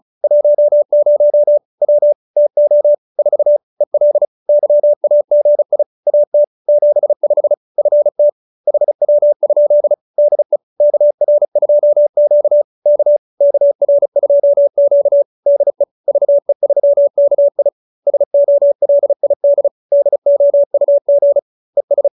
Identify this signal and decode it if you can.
Morse code, heard as a keyed tone, about 34 wpm, 580 Hz. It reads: 10 W TO 4 EL YAGI AT 75 FT HW? DE KR2Q K KR2Q DE VE3KI SOLID DOUG ES